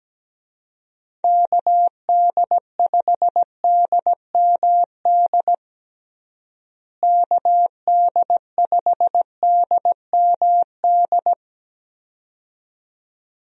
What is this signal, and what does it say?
Morse code, heard as a keyed tone, about 17 wpm, 700 Hz. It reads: KD5DMD KD5DMD
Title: USA Call Signs - 6 Characters - Repeated 2x - 17wpm